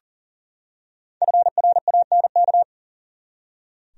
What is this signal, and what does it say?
Morse code, heard as a keyed tone, about 38 wpm, 700 Hz. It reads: FRANK
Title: USA Common Names - Max 5 Letters - 40wpm